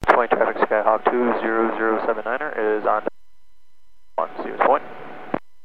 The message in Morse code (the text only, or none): none